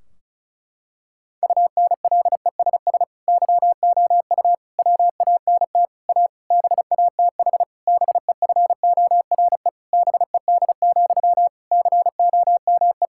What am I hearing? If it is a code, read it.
Morse code, heard as a keyed tone, about 36 wpm, 700 Hz. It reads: UNLESS YOU WANT A BATH BEFORE BED, COME
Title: Sentences from Top 1600 Words 35wpm